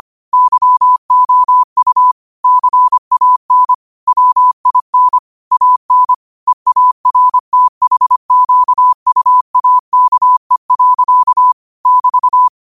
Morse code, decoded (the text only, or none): YOU CAN WIN AN EARTHQUAKE. <BT>